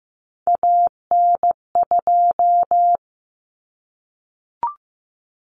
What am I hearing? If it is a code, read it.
Morse code, heard as a keyed tone, about 15 wpm, 700 Hz. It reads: AN2
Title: Common Cut Numbers - 3-Digit 15wpm